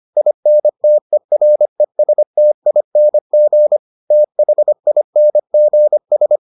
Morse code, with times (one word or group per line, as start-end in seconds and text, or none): 0.2-3.8 INTERESTING
4.1-6.4 THINGS